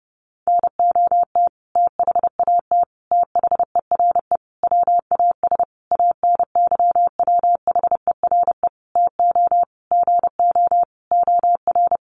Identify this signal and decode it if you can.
Morse code, heard as a keyed tone, about 30 wpm, 700 Hz. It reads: NOT THAT THERE WAS ANYWHERE TO GO OR
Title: Sentences from Top 1400 Words 30wpm